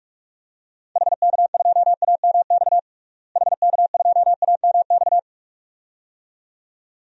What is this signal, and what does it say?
Morse code, heard as a keyed tone, about 46 wpm, 700 Hz. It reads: HK2AMX HK2AMX
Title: DXCC-100 Call Signs - 6 Characters - Repeated 2x - 45wpm